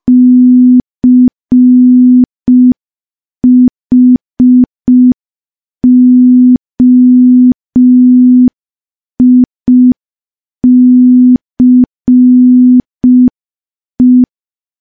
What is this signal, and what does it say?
Morse code, heard as a keyed tone, about 5 wpm, 250 Hz. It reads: CHOICE
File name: morse / morse-training-05WPM